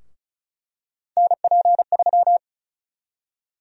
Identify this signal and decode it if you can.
Morse code, heard as a keyed tone, about 36 wpm, 700 Hz. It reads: NP3
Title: USA Call Sign Prefixes - Repeated 2x - 35wpm